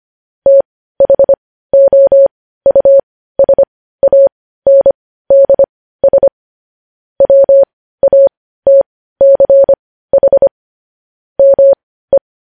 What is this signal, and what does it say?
Morse code, heard as a keyed tone, about 25 wpm, spaced as for 15 wpm, 550 Hz. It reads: THOUSANDS WATCH ME
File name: podcast / News headlines-1x-25x15